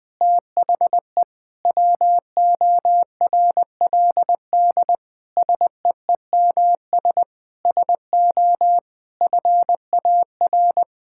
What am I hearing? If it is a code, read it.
Morse code, heard as a keyed tone, about 21 wpm, 700 Hz. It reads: THE WORLD SEEMS SO FAR
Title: Sentences with 5-Word Phrases - 20wpm